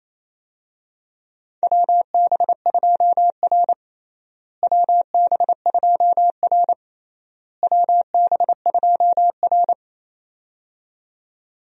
Morse code, heard as a keyed tone, about 28 wpm, 700 Hz. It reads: WB2R WB2R WB2R
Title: USA Call Signs - 4 Characters - Repeated 3x - 28wpm